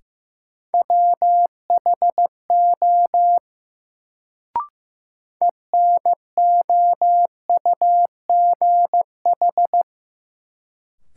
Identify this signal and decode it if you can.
Morse code, heard as a keyed tone, about 15 wpm, 700 Hz. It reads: WHO ENOUGH
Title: Sets of 2 Words - New Word Drill - 15wpm